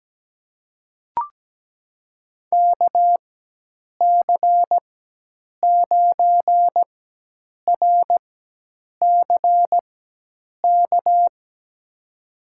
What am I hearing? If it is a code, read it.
Morse code, heard as a keyed tone, about 17 wpm, spaced as for 8 wpm, 700 Hz. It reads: KC9RCK
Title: USA Call Signs - 6 Characters - 4x Character Spacing - 17wpm